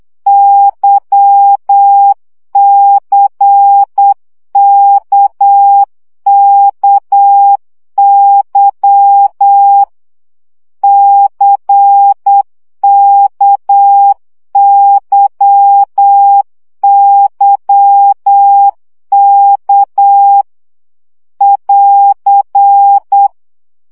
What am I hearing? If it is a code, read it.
Morse code, heard as a keyed tone, about 8 wpm, 800 Hz. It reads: YCKKY CKYYK <AR>
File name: CW gyakorlas k c y 7 baud